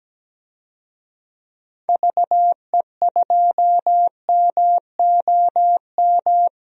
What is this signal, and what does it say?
Morse code, heard as a keyed tone, about 17 wpm, 700 Hz. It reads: VE2MOM
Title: DXCC-100 Call Signs - 6 Characters - 17wpm